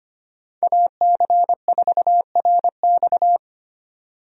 Morse code, heard as a keyed tone, about 25 wpm, 700 Hz. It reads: AC4RX
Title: USA Call Signs - 5 Characters - 25wpm